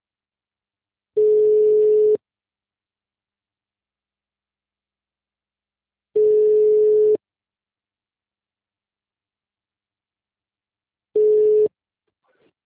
Ringback tone (Europe, 425 Hz).